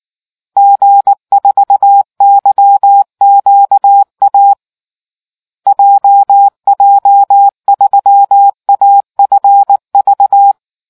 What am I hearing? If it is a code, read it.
Morse code, heard as a keyed tone, about 19 wpm, 800 Hz. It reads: G4YQA JJ3AFV